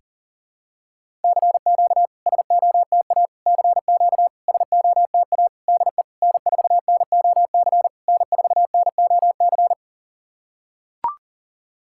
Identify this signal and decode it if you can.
Morse code, heard as a keyed tone, about 44 wpm, 700 Hz. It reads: CQ SOTA CQ SOTA DE N4NOC N4NOC
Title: SOTA QSO Practice 40wpm